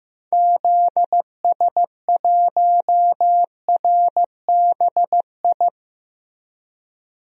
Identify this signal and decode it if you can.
Morse code, heard as a keyed tone, about 15 wpm, 700 Hz. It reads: ZS1RBI
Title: DXCC-100 Call Signs - 6 Characters - 15wpm